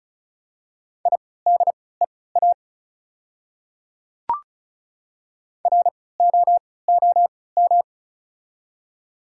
Morse code, heard as a keyed tone, about 36 wpm, spaced as for 20 wpm, 700 Hz. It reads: IDEA ROOM
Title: 3x Character Spacing - Top 300 Words - 35wpm